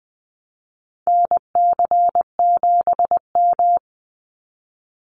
Morse code, heard as a keyed tone, about 20 wpm, 700 Hz. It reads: NC7M
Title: Call Signs Repeated 3x - 20wpm